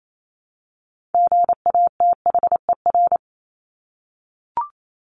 Morse code, heard as a keyed tone, about 28 wpm, 700 Hz. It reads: GATHER
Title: Top 1501-1750 Words 28wpm